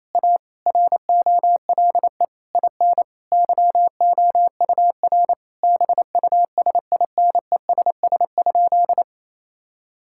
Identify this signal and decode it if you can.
Morse code, heard as a keyed tone, about 27 wpm, 700 Hz. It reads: A ROLE IN YOUR BUSINESS?
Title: Sentences from Top 1000 Words - Review Part 6 - 28wpm